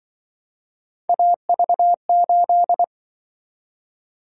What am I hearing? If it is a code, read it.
Morse code, heard as a keyed tone, about 25 wpm, 700 Hz. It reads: AV8
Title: Speed-Racing - Three Letter-Number Combinations - 20wpm